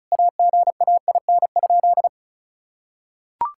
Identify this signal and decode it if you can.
Morse code, heard as a keyed tone, about 34 wpm, 700 Hz. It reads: AGAIN?
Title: Sentences from Top 1600 Words 35wpm